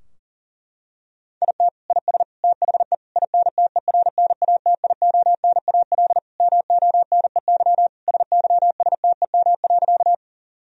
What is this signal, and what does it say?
Morse code, heard as a keyed tone, about 40 wpm, 700 Hz. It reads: IT IS THE INTERNATIONAL MONEY SYSTEM.
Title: Sentences from Top 1000 Words - Review Part 4 - 40wpm